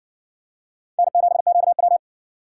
Morse code, heard as a keyed tone, about 55 wpm, 700 Hz. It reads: N7QW